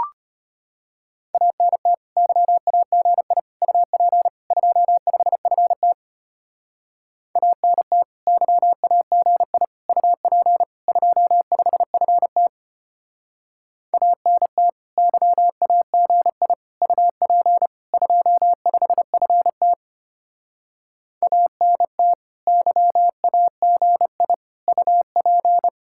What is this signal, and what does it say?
Morse code, heard as a keyed tone, about 33 wpm, 700 Hz. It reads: ANT YAGI UP 25FT ANT YAGI UP 25FT ANT YAGI UP 25FT ANT YAGI UP